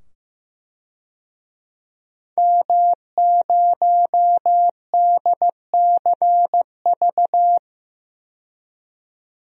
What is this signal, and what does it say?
Morse code, heard as a keyed tone, about 16 wpm, 700 Hz. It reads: M0DCV
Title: DXCC-100 Call Signs - 5 Characters - 15wpm